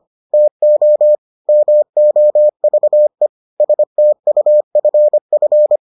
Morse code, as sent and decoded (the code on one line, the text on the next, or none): - --- / -- --- ...- . / ... - ..- ..-. ..-.
TO MOVE STUFF